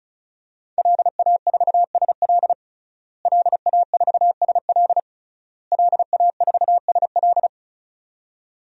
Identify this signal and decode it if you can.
Morse code, heard as a keyed tone, about 36 wpm, 690 Hz. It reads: LA4SL LA4SL LA4SL
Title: DXCC-100 Call Signs - 5 Characters - Repeated 3x - 35wpm